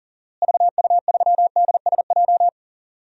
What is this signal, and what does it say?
Morse code, heard as a keyed tone, about 40 wpm, 700 Hz. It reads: VU3DSJ
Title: DXCC-100 Call Signs - 6 Characters - Repeated 3x - 40wpm